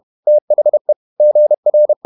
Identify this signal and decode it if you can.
Morse code, heard as a keyed tone, about 31 wpm, 600 Hz. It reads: THE GR